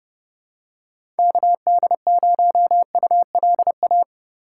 Morse code, heard as a keyed tone, about 30 wpm, 700 Hz. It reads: KD0ULA
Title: USA Call Signs - 6 Characters - 2x Character Spacing - 30wpm